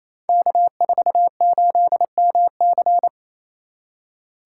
Morse code, heard as a keyed tone, about 28 wpm, 690 Hz. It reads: K48MC
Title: Code Groups - 5 Letter:Numbers - 28wpm